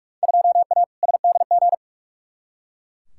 Morse code, heard as a keyed tone, about 42 wpm, 700 Hz. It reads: 2A SDG
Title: Field Day Exchanges Encore - 45wpm